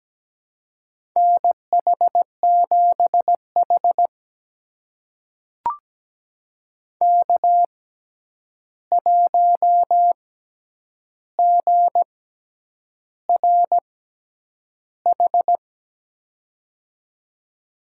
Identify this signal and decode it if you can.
Morse code, heard as a keyed tone, about 17 wpm, spaced as for 8 wpm, 700 Hz. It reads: NH7H K1GRH
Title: USA Call Signs - 6x Character Spacing - 17wpm